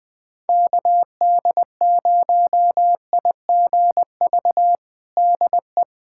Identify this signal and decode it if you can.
Morse code, heard as a keyed tone, about 20 wpm, 700 Hz. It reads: KD0IGV DE
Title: USA Callsign DE Callsign - 6 Characters - 20wpm